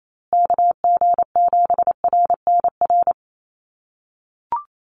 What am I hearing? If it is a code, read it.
Morse code, heard as a keyed tone, about 28 wpm, 700 Hz. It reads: KG7RNR
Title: USA Call Signs Repeated 2x 28wpm